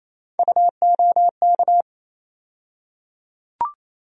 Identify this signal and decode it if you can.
Morse code, heard as a keyed tone, about 28 wpm, 700 Hz. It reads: UOK
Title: USA Call Sign Postfixes - 3 Letters 28wpm